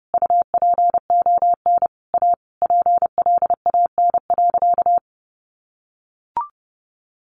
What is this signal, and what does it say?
Morse code, heard as a keyed tone, about 30 wpm, 700 Hz. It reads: UPON A PLAN.
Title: Sentences from Top 1200 Words 30wpm